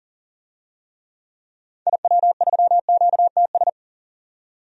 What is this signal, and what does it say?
Morse code, heard as a keyed tone, about 40 wpm, 700 Hz. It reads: IW3QTS
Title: DXCC-100 Call Signs - 6 Characters - 40wpm